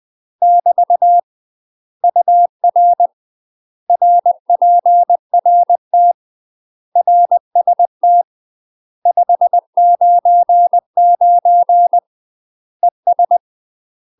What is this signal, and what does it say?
Morse code, heard as a keyed tone, about 20 wpm, 700 Hz. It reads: <BT> UR RPRT RST 599 ES
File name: UK MASTER QSOs TRAINING / 20WPM / UK MASTER QSOs 0029